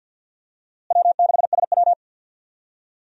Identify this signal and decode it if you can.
Morse code, heard as a keyed tone, about 52 wpm, 700 Hz. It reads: W6SW
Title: USA Call Signs - 4 Characters - Repeated 2x - 50wpm